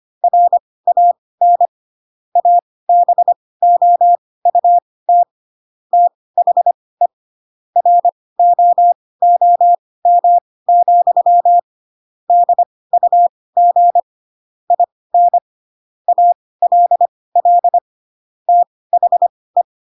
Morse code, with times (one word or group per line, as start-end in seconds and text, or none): 0.2-1.7 RAN
2.4-5.2 ABOUT
5.9-7.1 THE
7.8-11.6 ROOM,
12.3-14.0 DUG
14.7-15.4 IN
16.1-17.8 ALL
18.5-19.6 THE